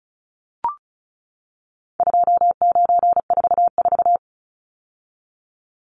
Morse code, heard as a keyed tone, about 34 wpm, 700 Hz. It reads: 2944
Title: Numbers - 4-Digit 35wpm